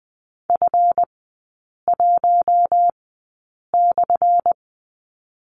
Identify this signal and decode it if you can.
Morse code, heard as a keyed tone, about 20 wpm, spaced as for 8 wpm, 700 Hz. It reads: F1/
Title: 205 Lesson 33e - Letter j - Three Characters 20wpm